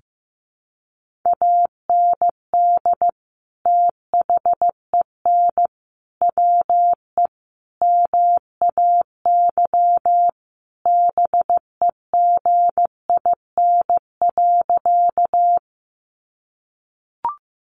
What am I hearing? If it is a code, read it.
Morse code, heard as a keyed tone, about 15 wpm, 700 Hz. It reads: AND THEN WE MAY BEGIN.
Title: Sentences from Top 300 Words 15wpm